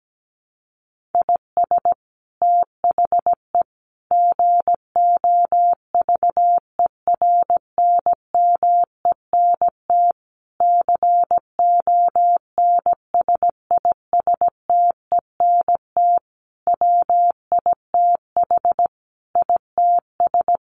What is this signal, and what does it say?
Morse code, heard as a keyed tone, about 17 wpm, 700 Hz. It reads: IS THE GOVERNMENT CONSISTENT WITH ITS
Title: Sentences from Top 1500 Words 17wpm